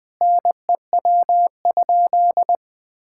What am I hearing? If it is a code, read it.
Morse code, heard as a keyed tone, about 20 wpm, 700 Hz. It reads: NEW?
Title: Sentences with 6-Word Phrases - 20wpm